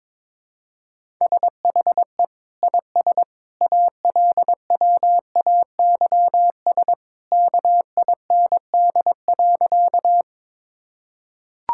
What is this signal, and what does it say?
Morse code, heard as a keyed tone, about 22 wpm, 700 Hz. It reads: SHE IS ALWAYS KIND.
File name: practice / 22wpm / Sentences with 2-Word Phrases - 22wpm